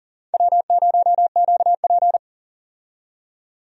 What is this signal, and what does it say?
Morse code, heard as a keyed tone, about 40 wpm, 700 Hz. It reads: W0QP
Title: USA Call Signs - 4 Characters - 40wpm